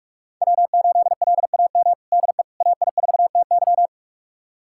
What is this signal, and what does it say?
Morse code, heard as a keyed tone, about 46 wpm, 700 Hz. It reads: W8LAM DE AI4TY
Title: USA Callsign DE Callsign - 5 Characters - 45wpm